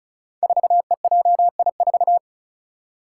Morse code, heard as a keyed tone, about 34 wpm, 700 Hz. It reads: 4EJI4
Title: Five Challenging Letter-Number Combinations 35wpm